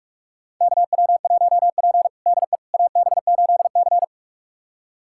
Morse code, heard as a keyed tone, about 46 wpm, 700 Hz. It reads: KW1P DE AB8C
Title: USA Callsign DE Callsign - 4 Characters - 45wpm